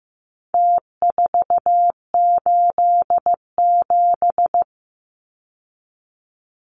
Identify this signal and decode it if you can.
Morse code, heard as a keyed tone, about 15 wpm, 700 Hz. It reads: T487
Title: Numbers - Contest Style - 4-Digit 15wpm